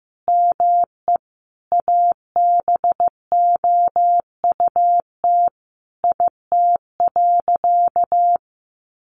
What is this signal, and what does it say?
Morse code, heard as a keyed tone, about 15 wpm, 700 Hz. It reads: ME ABOUT IT.
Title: Sentences from Top 1100 Words 15wpm